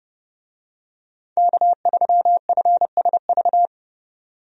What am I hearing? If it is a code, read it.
Morse code, heard as a keyed tone, about 30 wpm, 700 Hz. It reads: K3FSV
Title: USA Call Signs - 5 Characters - 30wpm